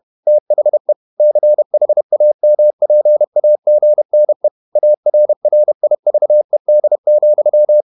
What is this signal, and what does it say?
Morse code, heard as a keyed tone, about 31 wpm, 600 Hz. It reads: THE CHAMPAGNE ARRIVED,